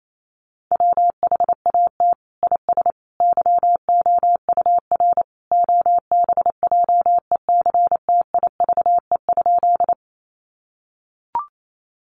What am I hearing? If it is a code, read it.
Morse code, heard as a keyed tone, about 28 wpm, 700 Hz. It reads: WHAT IS YOUR OBJECTIVE?
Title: Sentences from Top 1200 Words 28wpm